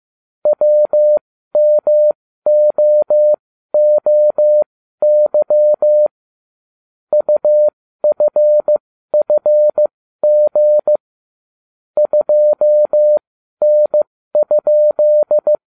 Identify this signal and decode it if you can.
Morse code, heard as a keyed tone, about 15 wpm, spaced as for 12 wpm, 600 Hz. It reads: WMOOY UFFG 2N?